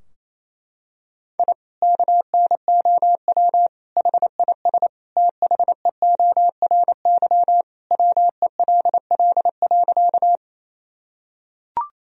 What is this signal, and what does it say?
Morse code, heard as a keyed tone, about 28 wpm, 700 Hz. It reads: I KNOW HIS THEORY WELL.
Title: Sentences from Top 1000 Words - Review Part 3 - 28wpm